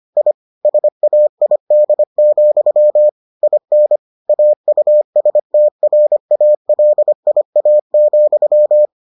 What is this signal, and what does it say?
Morse code, heard as a keyed tone, about 25 wpm, 600 Hz. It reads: I SAID, IN AUSTRALIA,